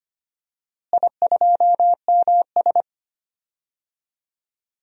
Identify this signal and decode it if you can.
Morse code, heard as a keyed tone, about 25 wpm, 700 Hz. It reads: I2MS
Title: DXCC-100 Call Signs - 4 Characters - Repeated 3x - 25wpm